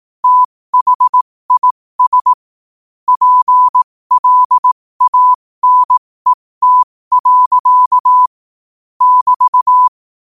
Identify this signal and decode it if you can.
Morse code, heard as a keyed tone, about 18 wpm, spaced as for 15 wpm, 1000 Hz. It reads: THIS PLANET. <BT>
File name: qotdfiles / 15wpm